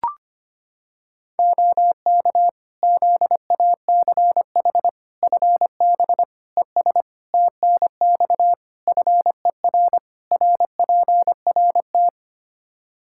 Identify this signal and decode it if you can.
Morse code, heard as a keyed tone, about 25 wpm, 700 Hz. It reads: OK ZACH FB ES TNX FER RPRT
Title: K1USN SST QSO Practice 25wpm